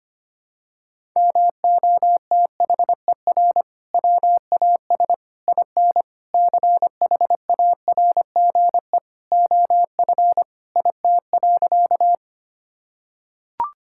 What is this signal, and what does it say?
Morse code, heard as a keyed tone, about 25 wpm, 700 Hz. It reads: MOTHER WAS IN CHARGE OF IT.
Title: Sentences from Top 1000 Words - Review Part 6 - 25wpm